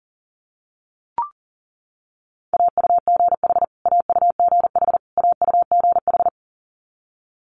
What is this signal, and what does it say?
Morse code, heard as a keyed tone, about 40 wpm, 700 Hz. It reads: AUGH AUGH AUGH
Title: Multi-letter Phonemes Repeated 3x 40wpm